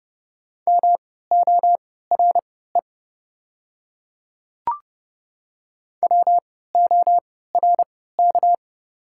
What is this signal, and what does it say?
Morse code, heard as a keyed tone, about 32 wpm, spaced as for 17 wpm, 700 Hz. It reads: MORE WORK
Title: 3x Character Spacing - Top 200 Words - 30wpm